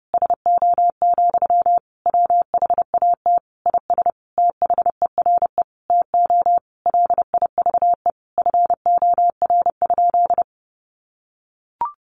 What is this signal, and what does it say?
Morse code, heard as a keyed tone, about 30 wpm, 700 Hz. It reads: SO, WHAT IS THERE TO LIVE FOR?
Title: Sentences from Top 200 Words 30wpm